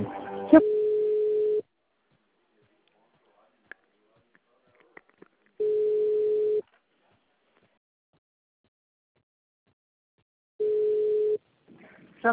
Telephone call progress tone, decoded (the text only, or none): ringback tone